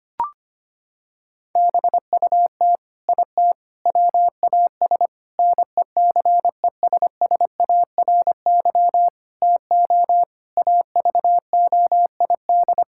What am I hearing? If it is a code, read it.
Morse code, heard as a keyed tone, about 25 wpm, 700 Hz. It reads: BUT IT WAS NECESSARY TO AVOID
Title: Sentences from Top 1000 Words - Review Part 6 - 25wpm